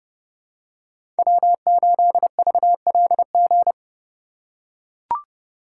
Morse code, heard as a keyed tone, about 30 wpm, 700 Hz. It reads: W8VLG
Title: USA Call Signs - 3x Character Spacing - 30wpm